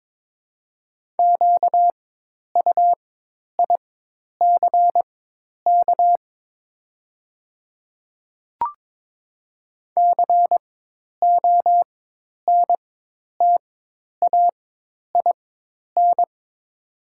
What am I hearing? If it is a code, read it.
Morse code, heard as a keyed tone, about 22 wpm, spaced as for 10 wpm, 690 Hz. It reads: QUICK CONTAIN
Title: 4x Character Spacing - Top 500 Words - 22wpm